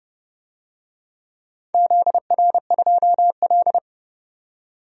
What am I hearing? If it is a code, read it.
Morse code, heard as a keyed tone, about 29 wpm, 700 Hz. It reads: ZR2L
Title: DXCC-100 Call Signs - 4 Characters - 30wpm